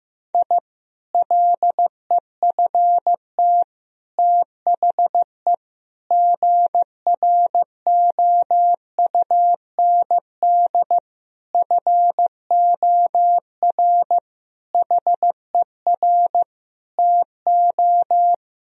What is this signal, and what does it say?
Morse code, heard as a keyed tone, about 15 wpm, 700 Hz. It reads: I LEFT THE GROUND FOR HER TO
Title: Sentences from Top 1000 Words - Review Part 2 - 15wpm